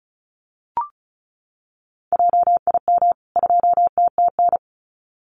Morse code, heard as a keyed tone, about 34 wpm, 700 Hz. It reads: JIM 2TTN
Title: CWT Exchange 35wpm